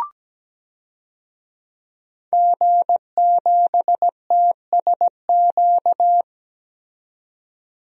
Morse code, heard as a keyed tone, about 17 wpm, 700 Hz. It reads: G7TSQ
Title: DXCC-100 Call Signs - 5 Characters - 17wpm